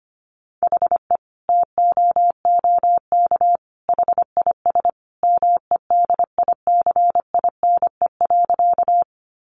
Morse code, heard as a keyed tone, about 25 wpm, 700 Hz. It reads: HE TOOK HIS MEDICINE.
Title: Sentences from Top 900 Words 25wpm